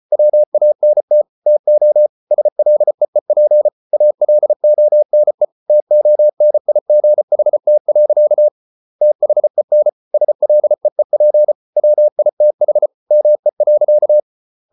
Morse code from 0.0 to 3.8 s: WANT TO SLEEP

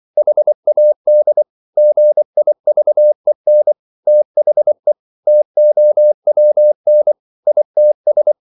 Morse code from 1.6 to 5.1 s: GIVEN THE

